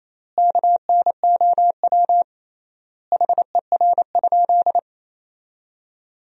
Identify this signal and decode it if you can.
Morse code, heard as a keyed tone, about 28 wpm, 700 Hz. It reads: KNOW HER?
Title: 3x Word-Spacing - Sentences from Top 100 Words - 28wpm